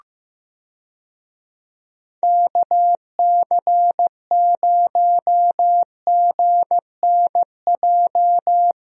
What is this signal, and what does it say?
Morse code, heard as a keyed tone, about 15 wpm, 700 Hz. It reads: KC0GNJ
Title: USA Call Signs - 6 Characters - Repeated 3x - 15wpm